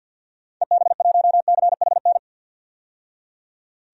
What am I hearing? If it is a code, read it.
Morse code, heard as a keyed tone, about 52 wpm, 700 Hz. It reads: EB1CHN
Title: DXCC-100 Call Signs - 6 Characters - 50wpm